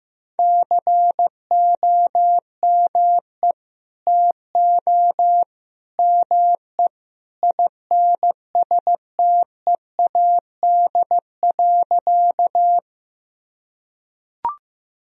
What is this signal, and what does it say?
Morse code, heard as a keyed tone, about 15 wpm, 700 Hz. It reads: COME TO ME INSTEAD.
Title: Sentences from Top 1000 Words - Review Part 4 - 15wpm